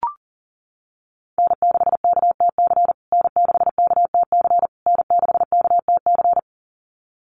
Morse code, heard as a keyed tone, about 40 wpm, 700 Hz. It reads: N6KTC N6KTC N6KTC
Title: USA Call Signs Repeated 3x 40wpm